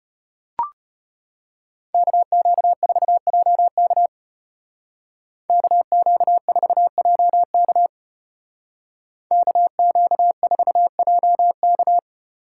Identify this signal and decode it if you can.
Morse code, heard as a keyed tone, about 32 wpm, 700 Hz. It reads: KQ4JK KQ4JK KQ4JK